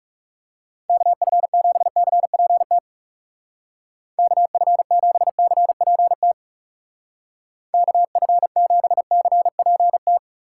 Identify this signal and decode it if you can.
Morse code, heard as a keyed tone, about 37 wpm, 700 Hz. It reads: KF7CPT KF7CPT KF7CPT
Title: Speed Racing - USA Call Signs 30wpm